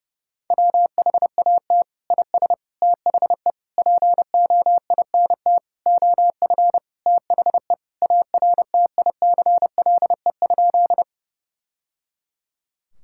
Morse code, heard as a keyed tone, about 32 wpm, 700 Hz. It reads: WHAT IS THE POINT OF THE ARTICLE?
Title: Sentences from Top 1000 Words - Review Part 5 - 30wpm